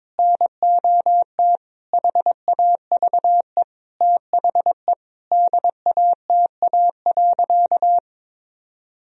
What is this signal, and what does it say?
Morse code, heard as a keyed tone, about 22 wpm, 700 Hz. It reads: NOT HAVE THE DATA.
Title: Sentences from Top 1000 Words - Review Part 3 - 22wpm